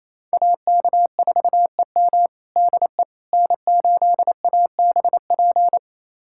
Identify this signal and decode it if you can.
Morse code, heard as a keyed tone, about 28 wpm, 700 Hz. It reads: AK4EM DE N8ABP
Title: USA Callsign DE Callsign - 5 Characters - 28wpm